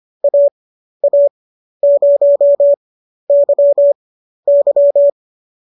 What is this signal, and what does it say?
Morse code, heard as a keyed tone, about 25 wpm, spaced as for 12 wpm, 590 Hz. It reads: AA0YY
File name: qso008 12